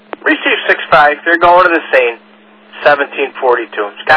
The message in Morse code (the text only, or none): none